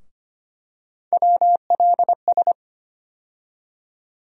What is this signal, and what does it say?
Morse code, heard as a keyed tone, about 25 wpm, 700 Hz. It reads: WLS
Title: USA Call Sign Postfixes - 3 Letters 25wpm